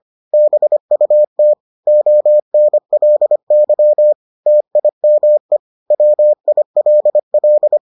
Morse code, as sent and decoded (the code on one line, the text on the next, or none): -... ..- - / --- -. .-.. -.-- / - .. -- . / .-- .. .-.. .-..
BUT ONLY TIME WILL